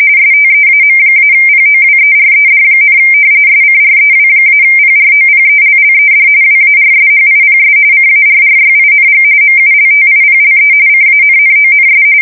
RTTY, about 45.45 baud. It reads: STEAM-POWERED, SCREW-PROPELLED WARSHIPS CARRYING MULTIPLE CANNONS
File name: fsk170 / battleshipnj